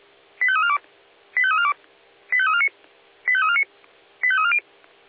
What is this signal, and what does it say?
Five-tone selective call (ZVEI), heard as quick runs of five tones, tones of about 70 ms. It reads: 85342 85342 85349 85349 85340